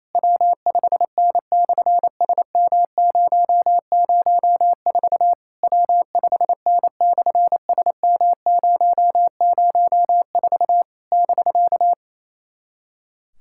Morse code, heard as a keyed tone, about 28 wpm, 700 Hz. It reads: W5N/SM004 W5N/SM004 <BK>
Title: SOTA QSO Practice 28wpm